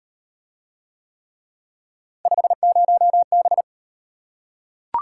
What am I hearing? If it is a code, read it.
Morse code, heard as a keyed tone, about 34 wpm, 700 Hz. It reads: 50B